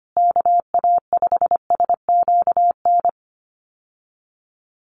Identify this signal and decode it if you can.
Morse code, heard as a keyed tone, about 25 wpm, 700 Hz. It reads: KA5SQN
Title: USA Call Signs Repeated 3x 25wpm